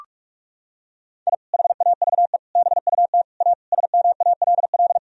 Morse code, heard as a keyed tone, about 41 wpm, 700 Hz. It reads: I HAVE BUT A SMALL
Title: Sentences from Top 1000 Words - Review Part 4 - 45wpm